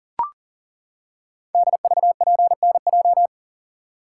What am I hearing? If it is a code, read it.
Morse code, heard as a keyed tone, about 39 wpm, 700 Hz. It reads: DVPNJ
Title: Code Groups - 5 Letters - 40wpm